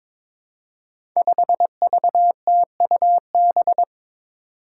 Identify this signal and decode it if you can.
Morse code, heard as a keyed tone, about 22 wpm, 700 Hz. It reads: 5VTUB
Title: Five Challenging Letter-Number Combinations 22wpm